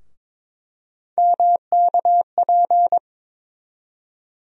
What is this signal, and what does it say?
Morse code, heard as a keyed tone, about 22 wpm, 700 Hz. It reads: MKP
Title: USA Call Sign Postfixes - 3 Letters 22wpm